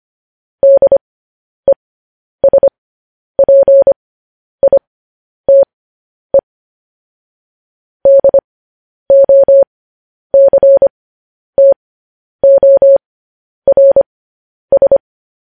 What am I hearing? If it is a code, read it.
Morse code, heard as a keyed tone, about 25 wpm, spaced as for 10 wpm, 550 Hz. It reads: DESPITE DOCTORS